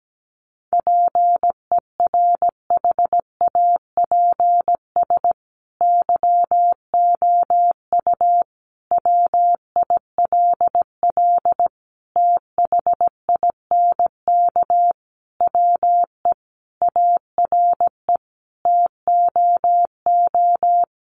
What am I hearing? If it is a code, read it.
Morse code, heard as a keyed tone, about 17 wpm, 700 Hz. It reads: PERHAPS YOU WILL THINK WE ARE TOO
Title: Sentences from Top 1400 Words 17wpm